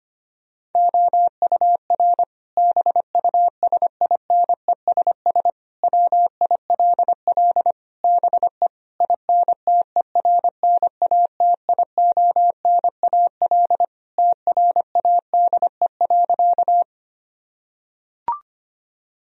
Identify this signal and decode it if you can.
Morse code, heard as a keyed tone, about 25 wpm, 700 Hz. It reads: OUR BUSINESS WILL BE INTERNATIONAL TRADE.
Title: Sentences from Top 1000 Words - Review Part 4 - 25wpm